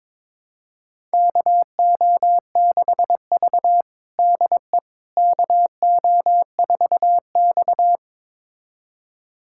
Morse code, heard as a keyed tone, about 22 wpm, 700 Hz. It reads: KO6V DE KO4X
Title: USA Callsign DE Callsign - 4 Characters - 22wpm